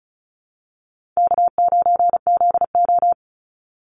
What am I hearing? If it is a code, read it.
Morse code, heard as a keyed tone, about 34 wpm, 700 Hz. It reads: K9ZO
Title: Call Signs - 35wpm